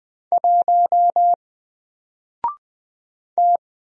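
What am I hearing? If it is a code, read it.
Morse code, heard as a keyed tone, about 20 wpm, 700 Hz. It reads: 1 T
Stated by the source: Single letter and number Morse drill at 20 wpm, from N4LCD.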